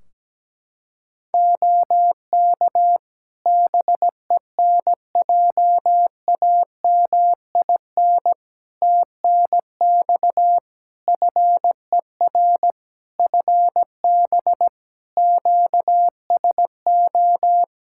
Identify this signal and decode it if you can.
Morse code, heard as a keyed tone, about 17 wpm, 700 Hz. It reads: OK BENJAMIN TNX FER FB QSO
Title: K1USN SST QSO Practice 17wpm